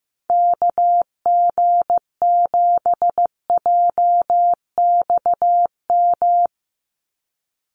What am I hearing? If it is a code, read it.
Morse code, heard as a keyed tone, about 15 wpm, 700 Hz. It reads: KG7JXM
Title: USA Call Signs 15wpm